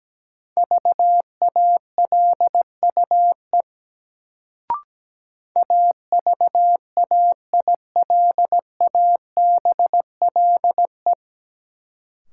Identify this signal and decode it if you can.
Morse code, heard as a keyed tone, about 17 wpm, 700 Hz. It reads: VALUE AVAILABLE
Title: Sentences from Top 600 Words - New Word Drill - 17wpm